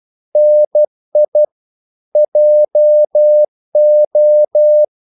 Morse code, heard as a keyed tone, about 12 wpm, 600 Hz. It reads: NI JO
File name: Korte ord DK 12wpm